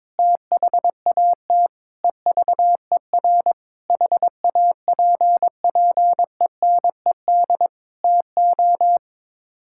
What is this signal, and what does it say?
Morse code, heard as a keyed tone, about 22 wpm, 700 Hz. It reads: THAT EVER HAPPENED TO
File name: practice / 22wpm / Sets of 8 Words - 22wpm